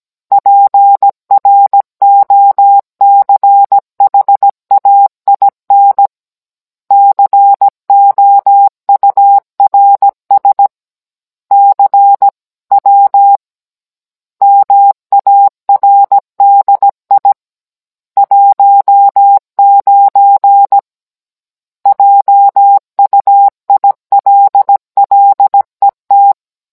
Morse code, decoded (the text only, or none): PROCHAIN COURS CW MARDI 19 JUILLET